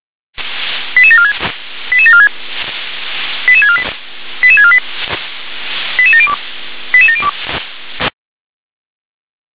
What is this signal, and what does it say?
Five-tone selective call (ZVEI), heard as quick runs of five tones, tones of about 70 ms. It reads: 88746 88746 88648 88648 88702 88702
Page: /357/SDR/ZVEI/